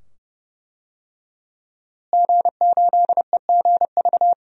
Morse code, heard as a keyed tone, about 31 wpm, 700 Hz. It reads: G8EGV
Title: DXCC-100 Call Signs - 5 Characters - Repeated 2x - 30wpm